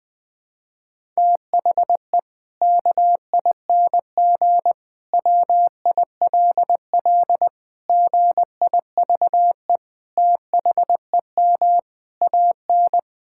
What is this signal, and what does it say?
Morse code, heard as a keyed tone, about 19 wpm, 700 Hz. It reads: THE KING WILL GIVE THEM AN
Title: Sentences from Top 1000 Words - Review Part 5 - 20wpm